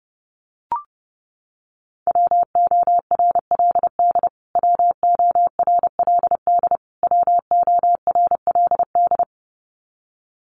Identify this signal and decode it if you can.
Morse code, heard as a keyed tone, about 28 wpm, 700 Hz. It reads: WORLD WORLD WORLD
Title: Top 300 Words Repeated 30wpm